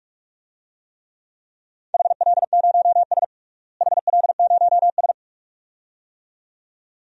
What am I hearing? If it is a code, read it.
Morse code, heard as a keyed tone, about 46 wpm, 700 Hz. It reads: HL0S HL0S
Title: DXCC-100 Call Signs - 4 Characters - Repeated 2x - 45wpm